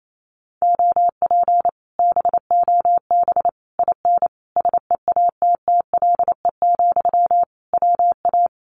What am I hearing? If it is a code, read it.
Morse code, heard as a keyed tone, about 28 wpm, 700 Hz. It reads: OP BOB IN SEATTLE, WA
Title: Common QSO Phrases Encore 28wpm